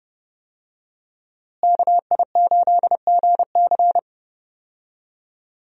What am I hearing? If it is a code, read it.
Morse code, heard as a keyed tone, about 32 wpm, 700 Hz. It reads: KI8GC
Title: USA Call Signs - 5 Characters - Repeated 3x - 30wpm